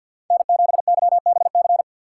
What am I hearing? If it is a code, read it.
Morse code, heard as a keyed tone, about 52 wpm, 700 Hz. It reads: N7YBC
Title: USA Call Signs - 5 Characters - 3x Character Spacing - 50wpm